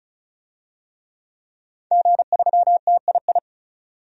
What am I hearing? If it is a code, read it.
Morse code, heard as a keyed tone, about 34 wpm, 700 Hz. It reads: G3TII
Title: DXCC-100 Call Signs - 5 Characters - 35wpm